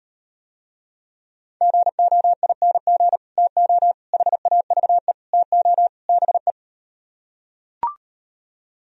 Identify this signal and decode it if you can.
Morse code, heard as a keyed tone, about 38 wpm, 700 Hz. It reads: GOING TO HAVE TO BE